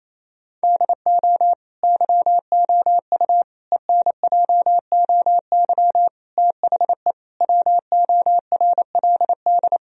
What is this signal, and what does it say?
Morse code, heard as a keyed tone, about 28 wpm, 700 Hz. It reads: DO YOU ENJOY THE WORLD
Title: Sentences from Top 1000 Words - Review Part 4 - 28wpm